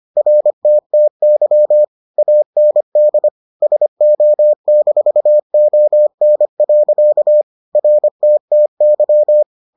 Morse code, heard as a keyed tone, about 25 wpm, 600 Hz. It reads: RTTY AND SO-ON. RTTY